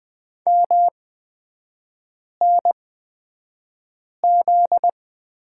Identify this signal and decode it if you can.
Morse code, heard as a keyed tone, about 20 wpm, spaced as for 5 wpm, 700 Hz. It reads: MNZ
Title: 243 Lesson 39e - Letter z - Three Characters 20wpm